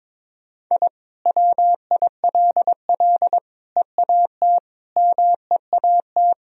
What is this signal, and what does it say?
Morse code, heard as a keyed tone, about 22 wpm, 700 Hz. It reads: I WILL EAT MEAT
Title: Sentences from Top 1000 Words - Review Part 3 - 22wpm